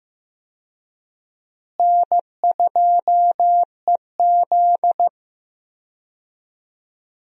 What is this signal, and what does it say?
Morse code, heard as a keyed tone, about 15 wpm, 700 Hz. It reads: N2EZ
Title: USA Call Signs - 4 Characters - Repeated 3x - 15wpm